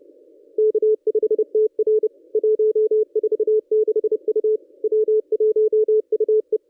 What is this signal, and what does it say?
Morse code, heard as a keyed tone, about 30 wpm, 440 Hz. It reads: K5TR 146 U W1UE
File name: contests / ss-cw-k5tr-op-n5rz -1100